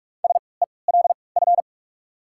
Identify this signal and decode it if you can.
Morse code, heard as a keyed tone, about 44 wpm, spaced as for 25 wpm, 700 Hz. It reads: SELF
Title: 3x Character Spacing - Top 200 Words - 45wpm